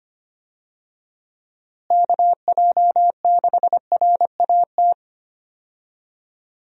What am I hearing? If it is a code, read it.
Morse code, heard as a keyed tone, about 25 wpm, 700 Hz. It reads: KJ6RAT
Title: USA Call Signs - 6 Characters - Repeated 3x - 25wpm